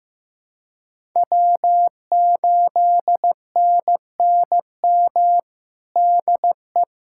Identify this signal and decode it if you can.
Morse code, heard as a keyed tone, about 15 wpm, 700 Hz. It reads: W8NNM DE
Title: USA Callsign DE Callsign - 5 Characters - 15wpm